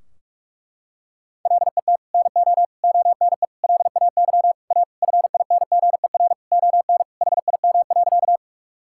Morse code, heard as a keyed tone, about 44 wpm, 700 Hz. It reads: LET NO ONE LAY A FINGER ON HIM.
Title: Sentences from Top 1600 Words 45wpm